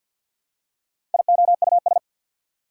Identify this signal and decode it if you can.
Morse code, heard as a keyed tone, about 48 wpm, 690 Hz. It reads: IOUS
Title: Prefixes and Suffixes 50wpm